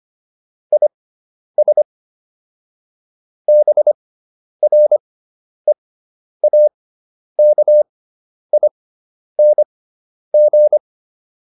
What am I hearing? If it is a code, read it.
Morse code, heard as a keyed tone, about 25 wpm, spaced as for 10 wpm, 600 Hz. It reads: IS BREAKING
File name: QSO 410 10